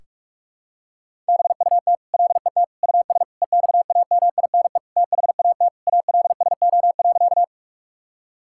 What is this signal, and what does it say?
Morse code, heard as a keyed tone, about 46 wpm, 700 Hz. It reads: BUT LET US EXAMINE THAT ALSO.
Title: Sentences from Top 1600 Words 45wpm